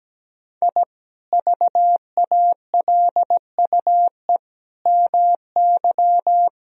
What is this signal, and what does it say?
Morse code, heard as a keyed tone, about 17 wpm, 700 Hz. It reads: I VALUE MY
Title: Sentences from Top 1000 Words - Review Part 3 - 17wpm